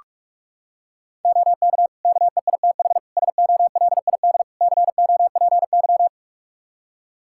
Morse code, heard as a keyed tone, about 41 wpm, 700 Hz. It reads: OK KEITH SOLID COPY
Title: Practice QSO - Exchange 3 - More Info - 45wpm